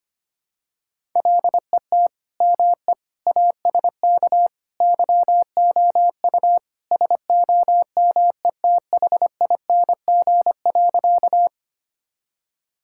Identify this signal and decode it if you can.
Morse code, heard as a keyed tone, about 25 wpm, 700 Hz. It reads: LET ME ASK YOU SOMETHING.
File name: practice / 25wpm / Sentences with 4-Word Phrases - 25wpm